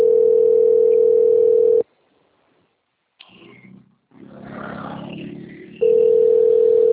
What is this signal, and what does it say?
Ringback tone (North America), 440 + 480 Hz.